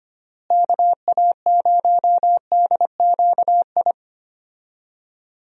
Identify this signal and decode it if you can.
Morse code, heard as a keyed tone, about 25 wpm, 700 Hz. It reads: KA0DQI
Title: USA Call Signs - 6 Characters - Repeated 2x - 25wpm